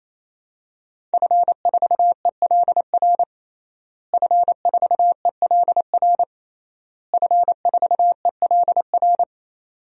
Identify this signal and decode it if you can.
Morse code, heard as a keyed tone, about 28 wpm, 700 Hz. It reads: F4ELR F4ELR F4ELR
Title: DXCC-100 Call Signs - 5 Characters - Repeated 3x - 28wpm